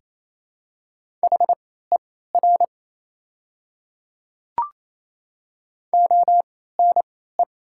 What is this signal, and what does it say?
Morse code, heard as a keyed tone, about 28 wpm, spaced as for 16 wpm, 700 Hz. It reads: HER ONE
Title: 3x Character Spacing - Top 100 Words - 28wpm